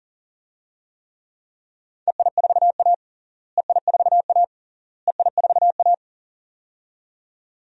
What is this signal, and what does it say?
Morse code, heard as a keyed tone, about 40 wpm, 700 Hz. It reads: EI4A EI4A EI4A
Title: DXCC-100 Call Signs - 4 Characters - Repeated 3x - 40wpm